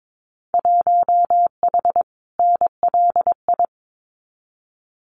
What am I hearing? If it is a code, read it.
Morse code, heard as a keyed tone, about 22 wpm, 700 Hz. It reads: 1H NLI
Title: Winter Field Day Exchanges 22wpm